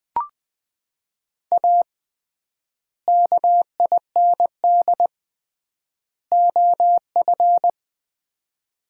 Morse code, heard as a keyed tone, about 20 wpm, 700 Hz. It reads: A KIND OF
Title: 3x Word-Spacing - Sets of 3 Words - 20wpm